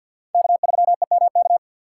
Morse code, heard as a keyed tone, about 47 wpm, 700 Hz. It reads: K3EMK
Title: USA Call Signs - 5 Characters - 2x Character Spacing - 50wpm